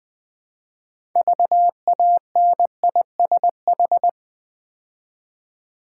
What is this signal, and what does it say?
Morse code, heard as a keyed tone, about 20 wpm, 700 Hz. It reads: VANISH
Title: Build A Word - 2 Letters At A Time - 6-Letter Words - 20wpm